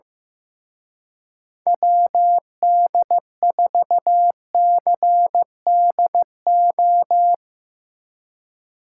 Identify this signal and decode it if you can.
Morse code, heard as a keyed tone, about 16 wpm, 700 Hz. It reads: WD4CDO
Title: Speed Racing - USA 2x3 Call Signs 15wpm